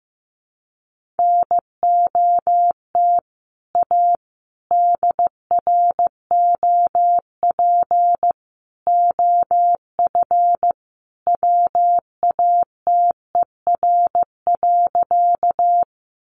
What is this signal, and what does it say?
Morse code, heard as a keyed tone, about 15 wpm, 700 Hz. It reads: NOT A DROP OF WATER.
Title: Sentences from Top 1400 Words 15wpm